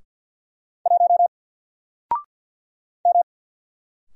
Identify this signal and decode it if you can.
Morse code, heard as a keyed tone, about 49 wpm, 700 Hz. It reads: 1 M
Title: Single Character (Letter:Number:Punctuation:Prosign) 50wpm